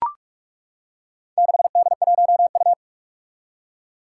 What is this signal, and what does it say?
Morse code, heard as a keyed tone, about 41 wpm, 700 Hz. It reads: 6D1U
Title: Four Challenging Letter-Number Combinations 45wpm